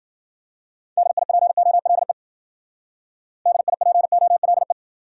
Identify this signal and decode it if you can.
Morse code, heard as a keyed tone, about 50 wpm, 700 Hz. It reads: DIPOLE DIPOLE